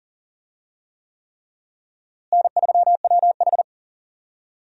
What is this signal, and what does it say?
Morse code, heard as a keyed tone, about 40 wpm, 700 Hz. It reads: N3WH
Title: USA Call Signs - 4 Characters - Repeated 3x - 40wpm